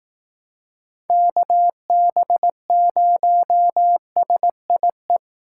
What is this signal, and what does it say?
Morse code, heard as a keyed tone, about 18 wpm, 700 Hz. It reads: KB0SIE
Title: Speed Racing - USA Call Signs 15wpm